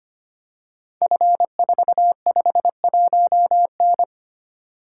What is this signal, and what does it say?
Morse code, heard as a keyed tone, about 25 wpm, 700 Hz. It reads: F451N
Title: Code Groups - 5 Letter:Numbers - 25wpm